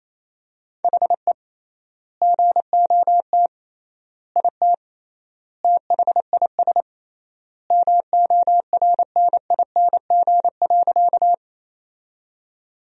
Morse code, heard as a keyed tone, about 27 wpm, 700 Hz. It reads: HE GOT IT THIS MORNING.
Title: 3x Word-Spacing - Sentences from Top 400 Words - 28wpm